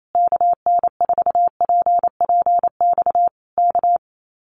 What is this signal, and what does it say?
Morse code, heard as a keyed tone, about 28 wpm, 700 Hz. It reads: KN4PPX K
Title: Common QSO Phrases Encore 28wpm